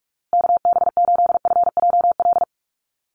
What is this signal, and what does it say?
Morse code, heard as a keyed tone, about 44 wpm, 700 Hz. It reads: KB8FJL
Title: USA Call Signs Repeated 3x 45wpm